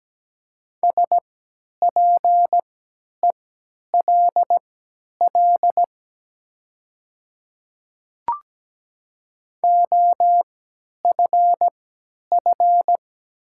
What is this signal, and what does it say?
Morse code, heard as a keyed tone, about 17 wpm, spaced as for 10 wpm, 700 Hz. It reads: SPELL OFF
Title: 3x Character Spacing - Top 500 Words - 17wpm